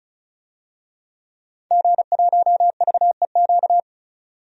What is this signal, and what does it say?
Morse code, heard as a keyed tone, about 36 wpm, 700 Hz. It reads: G1VEQ
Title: DXCC-100 Call Signs - 5 Characters - 35wpm